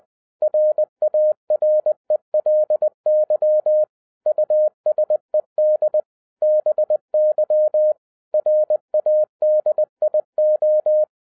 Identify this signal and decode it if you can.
Morse code, heard as a keyed tone, about 20 wpm, 610 Hz. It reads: RARELY USED BY RADIO